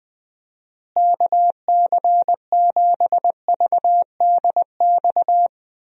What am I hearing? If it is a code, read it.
Morse code, heard as a keyed tone, about 20 wpm, 700 Hz. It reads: KC7VDX
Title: USA Call Signs - 6 Characters - Repeated 3x - 20wpm